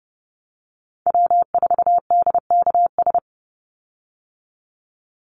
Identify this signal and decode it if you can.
Morse code, heard as a keyed tone, about 28 wpm, 700 Hz. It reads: W4DKS
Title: Call Signs Repeated 3x - 30wpm